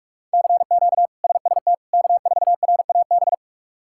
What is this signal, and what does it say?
Morse code, heard as a keyed tone, about 46 wpm, 700 Hz. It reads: CQ SST K4RAB
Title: K1USN SST QSO Practice - 45wpm